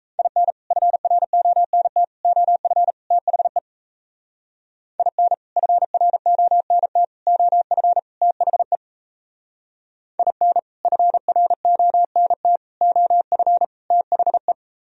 Morse code, heard as a keyed tone, about 38 wpm, 700 Hz. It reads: IN FRONT OF THE IN FRONT OF THE IN FRONT OF THE